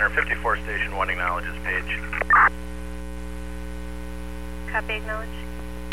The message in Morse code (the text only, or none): none